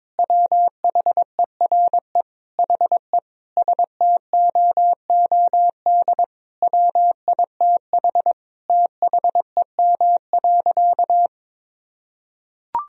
Morse code, heard as a keyed tone, about 22 wpm, 700 Hz. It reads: WHERE HE STOOD WITH THEM.
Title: Sentences from Top 1000 Words - Review Part 2 - 22wpm